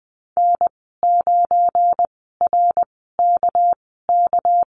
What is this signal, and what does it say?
Morse code, heard as a keyed tone, about 20 wpm, spaced as for 14 wpm, 700 Hz. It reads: N9RKK
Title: 213 Lesson 34g - Number 8 - Call Signs 20wpm